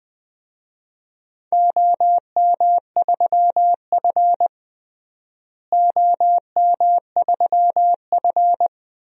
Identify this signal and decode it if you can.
Morse code, heard as a keyed tone, about 20 wpm, 700 Hz. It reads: OM3F OM3F
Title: DXCC-100 Call Signs - 4 Characters - Repeated 2x - 20wpm